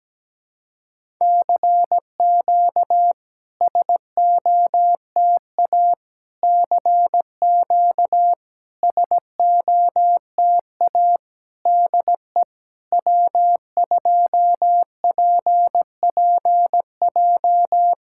Morse code, heard as a keyed tone, about 17 wpm, 690 Hz. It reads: CQ SOTA CQ SOTA DE W2PPJ
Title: SOTA QSO Practice 17wpm